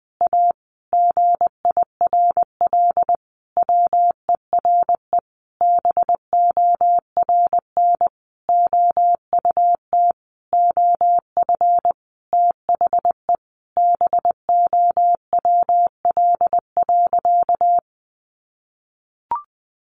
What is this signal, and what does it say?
Morse code, heard as a keyed tone, about 20 wpm, 700 Hz. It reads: A GIRL WERE BORN OUT OF THE BOWL.
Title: Sentences from Top 1200 Words 20wpm